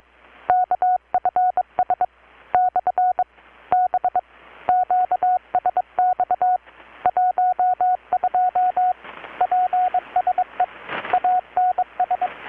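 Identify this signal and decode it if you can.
Morse code, heard as a keyed tone, about 22 wpm, 720 Hz, with radio noise. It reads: KFS / B QSX 12 PSE ANS
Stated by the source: Morse from KPH's Night of Nights XX maritime radiotelegraphy event, received in Chiapas, Mexico about 3,800 km away on a Kenwood R-600 with a grounded 10 m long-wire antenna.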